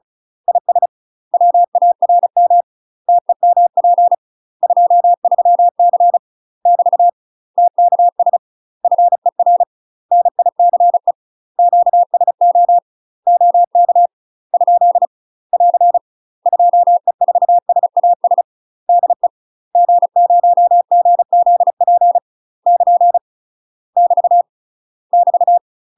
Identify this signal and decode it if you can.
Morse code, heard as a keyed tone, about 35 wpm, 700 Hz. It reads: IS WARM TEMP 23C <BT> TKS FER NICE QSO OK ? <AR> 2E4SAS DE G0GZP <KN> <BT> <BT>
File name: UK MASTER QSOs TRAINING / 35WPM / UK MASTER QSOs 0009